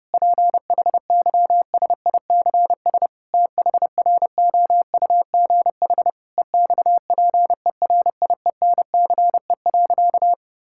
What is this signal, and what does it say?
Morse code, heard as a keyed tone, about 30 wpm, 700 Hz. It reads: PHYSICS THROUGH EXPERIENCE.
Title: Sentences from Top 1000 Words - Review Part 5 - 30wpm